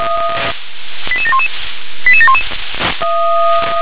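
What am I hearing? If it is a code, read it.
Five-tone selective call (ZVEI), heard as quick runs of five tones, tones of about 70 ms. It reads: 88711 88711
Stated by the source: ZVEI five-tone call 88704 received by SDR.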